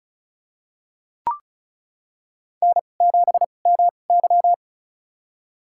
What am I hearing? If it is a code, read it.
Morse code, heard as a keyed tone, about 34 wpm, spaced as for 25 wpm, 700 Hz. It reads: N7MY
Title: USA Call Signs - 4 Characters - 2x Character Spacing - 35wpm